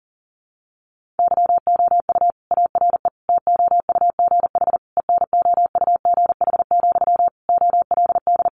Morse code, heard as a keyed tone, about 40 wpm, 700 Hz. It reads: YOU ARE TOUGH ENOUGH, OLD
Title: Sentences from Top 1200 Words 40wpm